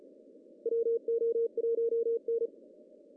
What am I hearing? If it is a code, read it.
Morse code, heard as a keyed tone, about 42 wpm, 460 Hz. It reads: WO1N
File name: CW / cqww-cw-k5tr